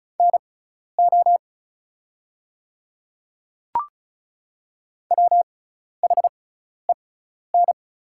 Morse code, heard as a keyed tone, about 34 wpm, spaced as for 12 wpm, 700 Hz. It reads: NO WHEN
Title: 6x Character Spacing - Top 100 Words - 35wpm